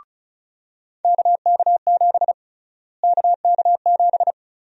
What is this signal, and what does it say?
Morse code, heard as a keyed tone, about 36 wpm, 700 Hz. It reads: KK7 KK7
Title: USA Call Sign Prefixes - Repeated 2x - 35wpm